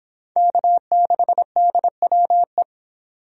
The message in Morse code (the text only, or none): K6DWE